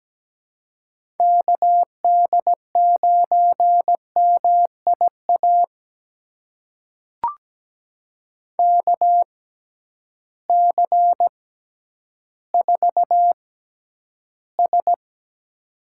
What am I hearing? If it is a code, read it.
Morse code, heard as a keyed tone, about 17 wpm, spaced as for 10 wpm, 700 Hz. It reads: KD9MIA KC4S
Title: USA Call Signs - 6 Characters - 6x Character Spacing - 17wpm